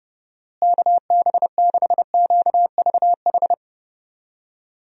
Morse code, heard as a keyed tone, about 32 wpm, 700 Hz. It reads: KB6QVH
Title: USA Call Signs - 6 Characters - Repeated 3x - 30wpm